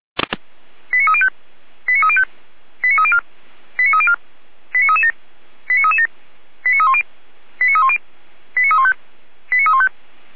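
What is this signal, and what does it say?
Five-tone selective call (ZVEI), heard as quick runs of five tones, tones of about 80 ms. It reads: 89305 89305 89304 89304 89337 89337 89310 89310 89315 89315
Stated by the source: ZVEI five-tone call 89265 received on SDR.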